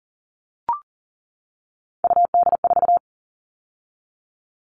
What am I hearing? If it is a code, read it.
Morse code, heard as a keyed tone, about 40 wpm, 700 Hz. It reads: UD4
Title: All Cut Numbers - 3-Digit 40wpm